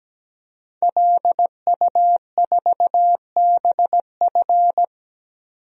Morse code, heard as a keyed tone, about 17 wpm, 700 Hz. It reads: LU4BF
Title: DXCC-100 Call Signs - 5 Characters - 17wpm